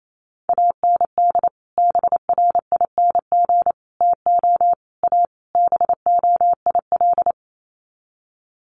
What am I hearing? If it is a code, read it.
Morse code, heard as a keyed tone, about 28 wpm, 700 Hz. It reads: AND BRING TO A BOIL
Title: Sets of 5 Words Encore 28wpm